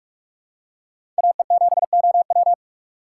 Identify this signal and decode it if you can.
Morse code, heard as a keyed tone, about 44 wpm, 700 Hz. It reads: AE7OW
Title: USA Call Signs - 4x Character Spacing - 45wpm